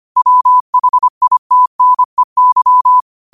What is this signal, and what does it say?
Morse code, heard as a keyed tone, about 25 wpm, 1000 Hz. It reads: WHITNEY
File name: qotdfiles / 25wpm